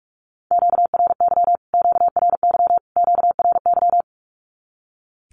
Morse code, heard as a keyed tone, about 44 wpm, 700 Hz. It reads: QRY QRY QRY
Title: Q-Codes Repeated 3x 45wpm